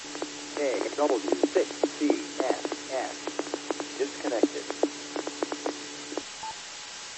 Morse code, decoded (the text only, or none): none